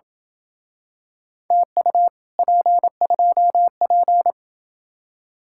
Morse code, heard as a keyed tone, about 28 wpm, 700 Hz. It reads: TU P2P